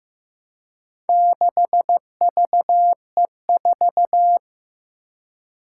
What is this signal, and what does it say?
Morse code, heard as a keyed tone, about 15 wpm, 700 Hz. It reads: 6VE4
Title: Four Challenging Letter-Number Combinations 15wpm